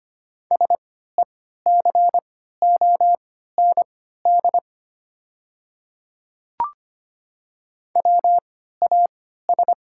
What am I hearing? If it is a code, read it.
Morse code, heard as a keyed tone, about 25 wpm, spaced as for 14 wpm, 700 Hz. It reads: SECOND WAS
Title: 3x Character Spacing - Top 300 Words - 25wpm